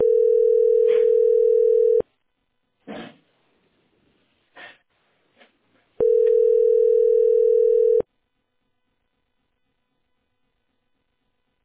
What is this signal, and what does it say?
Ringback tone (North America), 440 + 480 Hz.